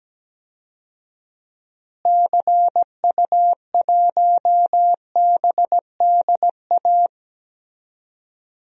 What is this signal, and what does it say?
Morse code, heard as a keyed tone, about 17 wpm, 700 Hz. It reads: CU1BDA
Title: DXCC-100 Call Signs - 6 Characters - Repeated 2x - 17wpm